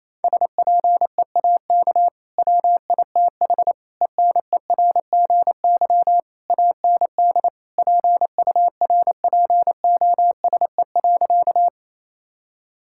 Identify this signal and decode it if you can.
Morse code, heard as a keyed tone, about 28 wpm, 700 Hz. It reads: SPEAK WITH ENERGY AND PURPOSE.
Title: Sentences from Top 1000 Words - Review Part 6 - 28wpm